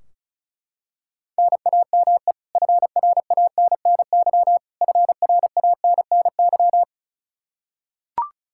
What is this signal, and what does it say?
Morse code, heard as a keyed tone, about 36 wpm, 700 Hz. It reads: NAME FRANNY FRANNY
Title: Practice QSO - Exchange 1 - Beginning - 35wpm